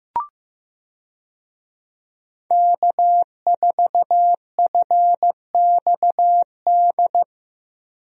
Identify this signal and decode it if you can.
Morse code, heard as a keyed tone, about 15 wpm, 700 Hz. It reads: K4FXD
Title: USA Call Signs - 5 Characters - Repeated 3x - 15wpm